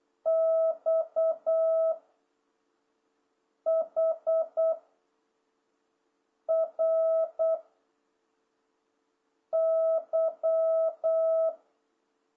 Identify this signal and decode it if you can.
Morse code, heard as a keyed tone, about 8 wpm, 660 Hz. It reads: X H R Y